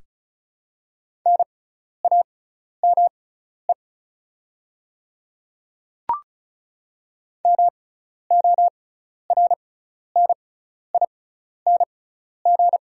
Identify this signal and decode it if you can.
Morse code, heard as a keyed tone, about 36 wpm, spaced as for 12 wpm, 700 Hz. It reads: NAME MORNING
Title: 6x Character Spacing - Top 500 Words - 35wpm